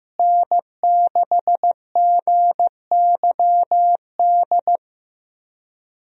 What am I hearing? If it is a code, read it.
Morse code, heard as a keyed tone, about 15 wpm, 700 Hz. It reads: N6GYD
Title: USA Call Signs - 5 Characters - Repeated 2x - 15wpm